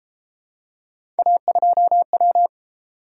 Morse code, heard as a keyed tone, about 31 wpm, 700 Hz. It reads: A2W